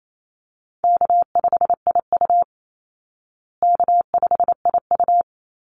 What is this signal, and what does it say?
Morse code, heard as a keyed tone, about 28 wpm, 700 Hz. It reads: K5IU K5IU
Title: Easy Call Signs Repeated 2x - 28wpm